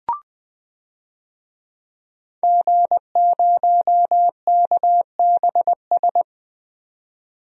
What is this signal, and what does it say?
Morse code, heard as a keyed tone, about 20 wpm, 700 Hz. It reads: G0KBS
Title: DXCC-100 Call Signs - 5 Characters - 20wpm